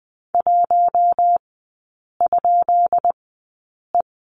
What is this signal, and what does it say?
Morse code, heard as a keyed tone, about 20 wpm, spaced as for 8 wpm, 700 Hz. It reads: 1?E
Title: 184 Lesson 30e - Letter b - Three Characters 20wpm